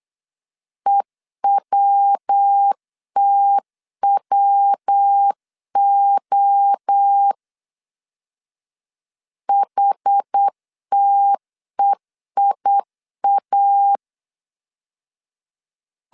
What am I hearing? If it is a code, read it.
Morse code, heard as a keyed tone, about 8 wpm, 800 Hz. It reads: EWTWO HTEIA